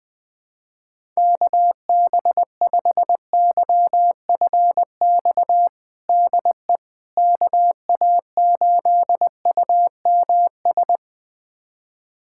Morse code, heard as a keyed tone, about 20 wpm, 700 Hz. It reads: KB5YFX DE KA8UMS
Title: USA Callsign DE Callsign - 6 Characters - 20wpm